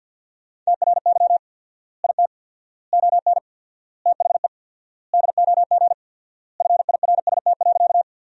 Morse code, heard as a keyed tone, about 50 wpm, 700 Hz. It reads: TRY IT ON THE DOG FIRST.
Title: 4x Word-Spacing - Sentences from Top 400 Words - 50wpm